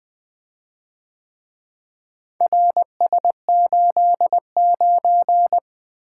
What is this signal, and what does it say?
Morse code, heard as a keyed tone, about 24 wpm, 700 Hz. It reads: RS89
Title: Speed-Racing - Four Letter-Number Combinations - 20wpm